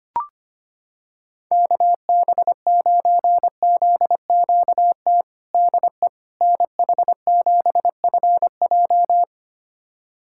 Morse code, heard as a keyed tone, about 25 wpm, 700 Hz. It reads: KB9ZQT DE NH7FJ
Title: USA Callsign DE Callsign 25wpm